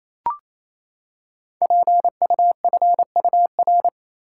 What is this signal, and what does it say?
Morse code, heard as a keyed tone, about 29 wpm, 700 Hz. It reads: PUFUR
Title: Code Groups - 5 Letters - 28wpm